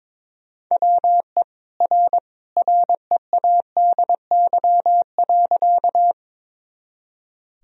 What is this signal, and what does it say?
Morse code, heard as a keyed tone, about 22 wpm, 700 Hz. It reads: WE R READY.
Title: Sentences from Top 400 Words and Abbreviations 22wpm